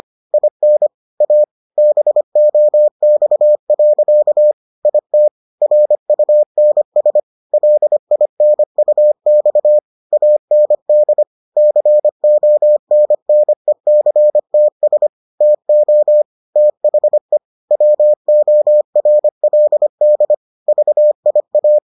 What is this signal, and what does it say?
Morse code, heard as a keyed tone, about 25 wpm, 600 Hz. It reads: IN A BOX. IT RUNS LINUX AND CONNECTS TO THE WORLD VIA